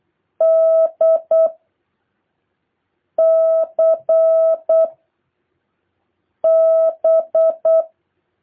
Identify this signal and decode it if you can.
Morse code, heard as a keyed tone, about 8 wpm, 660 Hz. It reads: D C B